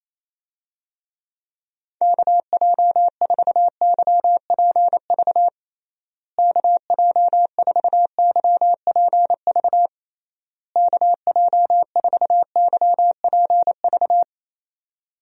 Morse code, heard as a keyed tone, about 28 wpm, 700 Hz. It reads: KJ4YPV KJ4YPV KJ4YPV
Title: USA Call Signs - 6 Characters - Repeated 3x - 28wpm